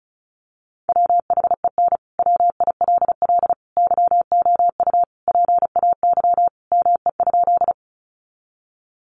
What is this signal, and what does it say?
Morse code, heard as a keyed tone, about 34 wpm, 700 Hz. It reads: WHEN WILL YOU PAY ME?
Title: Sentences from Top 700 Words 35wpm